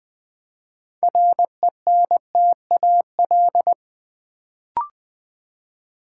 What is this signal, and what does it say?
Morse code, heard as a keyed tone, about 20 wpm, 700 Hz. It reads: RENTAL
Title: Build A Word - 2 Letters At A Time - 6-Letter Words - 20wpm